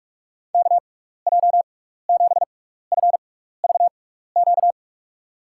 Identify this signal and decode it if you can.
Morse code, heard as a keyed tone, about 43 wpm, spaced as for 15 wpm, 700 Hz. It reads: KJ7FVQ
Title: USA Call Signs - 6 Characters - 6x Character Spacing - 45wpm